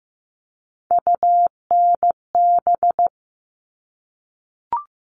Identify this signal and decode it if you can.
Morse code, heard as a keyed tone, about 15 wpm, 700 Hz. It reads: UNB
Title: All Cut Numbers - 3-Digit 15wpm